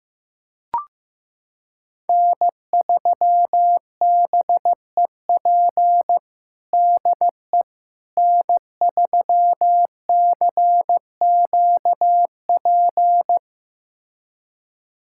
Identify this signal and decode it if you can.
Morse code, heard as a keyed tone, about 15 wpm, 700 Hz. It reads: N3BEP DE N3CQP
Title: USA Callsign DE Callsign - 5 Characters - 15wpm